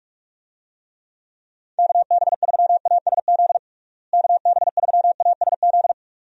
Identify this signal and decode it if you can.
Morse code, heard as a keyed tone, about 46 wpm, 700 Hz. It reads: KB3ASZ KB3ASZ
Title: USA Call Signs - 6 Characters - Repeated 2x - 45wpm